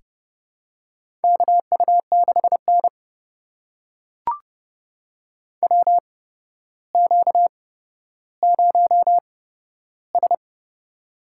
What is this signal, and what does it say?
Morse code, heard as a keyed tone, about 30 wpm, spaced as for 13 wpm, 700 Hz. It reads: KU6N WQ0S
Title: USA Call Signs - 4 Characters - 8x Character Spacing - 30wpm